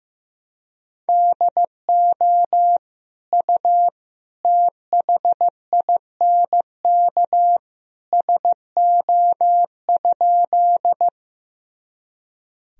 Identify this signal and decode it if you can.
Morse code, heard as a keyed tone, about 16 wpm, 700 Hz. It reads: DO U THINK SO?
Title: Sentences from Top 200 Words and Abbreviations 15wpm